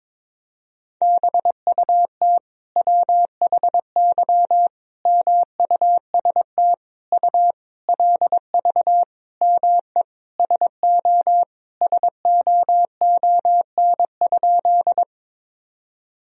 Morse code, heard as a keyed tone, about 22 wpm, 700 Hz. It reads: BUT WHY MUST U LV ME SO SOON?
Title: Sentences from Top 400 Words and Abbreviations 22wpm